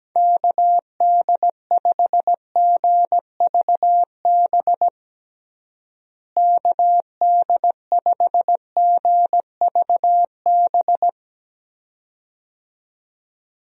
Morse code, heard as a keyed tone, about 17 wpm, 700 Hz. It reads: KD5GVB KD5GVB
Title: USA Call Signs - 6 Characters - Repeated 2x - 17wpm